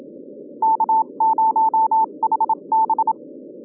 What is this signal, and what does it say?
Morse code, heard as a keyed tone, about 31 wpm, spaced as for 26 wpm, 890 Hz. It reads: K0HB